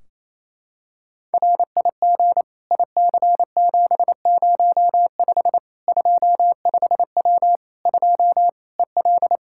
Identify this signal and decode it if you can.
Morse code, heard as a keyed tone, about 28 wpm, 700 Hz. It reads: RIG IC705 25W 2 EL
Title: Practice QSO - Exchange 2 - Info - 28wpm